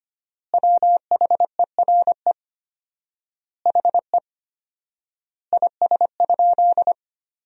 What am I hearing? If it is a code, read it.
Morse code, heard as a keyed tone, about 25 wpm, 700 Hz. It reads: WHERE HE IS?
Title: 4x Word-Spacing - Sentences from Top 300 Words - 25wpm